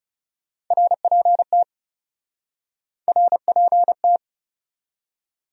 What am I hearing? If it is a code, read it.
Morse code, heard as a keyed tone, about 40 wpm, 700 Hz. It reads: RPT RPT